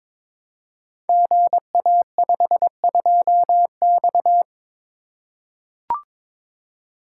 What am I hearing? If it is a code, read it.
Morse code, heard as a keyed tone, about 22 wpm, 700 Hz. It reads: GA52X
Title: Code Groups - 5 Letter:Numbers - 22wpm